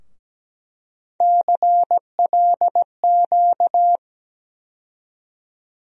Morse code, heard as a keyed tone, about 17 wpm, 700 Hz. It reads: CLQ
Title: USA Call Sign Postfixes - 3 Letters 17wpm